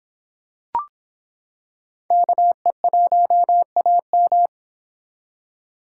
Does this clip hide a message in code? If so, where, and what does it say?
Yes: Morse code from 2.1 to 4.5 s, heard as a keyed tone, about 26 wpm, 700 Hz. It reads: KE1AM